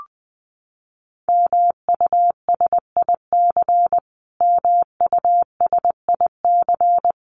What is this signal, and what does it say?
Morse code, heard as a keyed tone, about 19 wpm, 700 Hz. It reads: MUSIC MUSIC
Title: Top 300 Words Repeated 20wpm